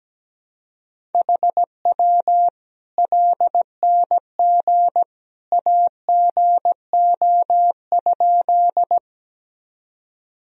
Morse code, heard as a keyed tone, about 17 wpm, 700 Hz. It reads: HW LNG AGO?
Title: Sentences from Top 500 Words and Abbreviations 17wpm